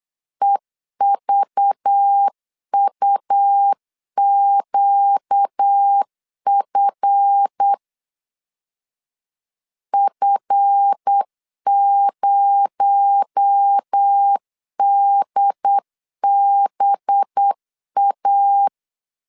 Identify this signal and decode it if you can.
Morse code, heard as a keyed tone, about 8 wpm, 800 Hz. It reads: EVUQF F0DBA
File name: LEZIONI / 7 GF9Z / 7 GF 8wpm 3108